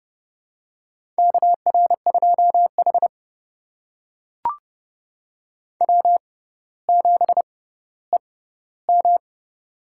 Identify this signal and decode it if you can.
Morse code, heard as a keyed tone, about 30 wpm, spaced as for 15 wpm, 700 Hz. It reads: KR2H W7EM
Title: USA Call Signs - 4 Characters - 6x Character Spacing - 30wpm